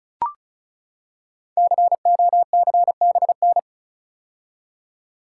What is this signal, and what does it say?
Morse code, heard as a keyed tone, about 36 wpm, 690 Hz. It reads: COCBN
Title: Code Groups - 5 Letters - 35wpm